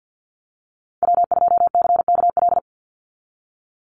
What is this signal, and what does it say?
Morse code, heard as a keyed tone, about 50 wpm, 700 Hz. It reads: W2CKL
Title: NA SOTA Activator Callsigns 50wpm